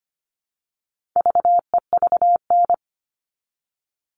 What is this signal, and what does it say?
Morse code, heard as a keyed tone, about 25 wpm, 700 Hz. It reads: VEVN
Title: All Cut Numbers - 4-Digit 25wpm